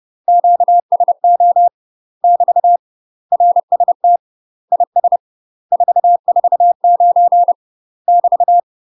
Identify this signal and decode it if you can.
Morse code, heard as a keyed tone, about 30 wpm, 700 Hz. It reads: QSO <BT> RST IS 449 <BT>
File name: UK MASTER QSOs TRAINING / 30WPM / UK MASTER QSOs 0014